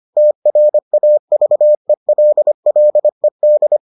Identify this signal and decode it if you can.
Morse code, heard as a keyed tone, about 25 wpm, 600 Hz. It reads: TRAVELLED